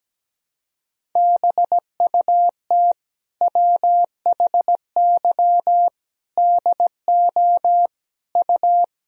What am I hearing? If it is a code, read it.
Morse code, heard as a keyed tone, about 17 wpm, 700 Hz. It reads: BUT WHY DO U
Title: Sentences from Top 200 Words and Abbreviations 17wpm